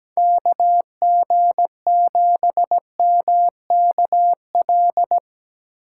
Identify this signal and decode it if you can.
Morse code, heard as a keyed tone, about 19 wpm, 700 Hz. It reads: KG7MKL
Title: Speed Racing - USA Call Signs 17wpm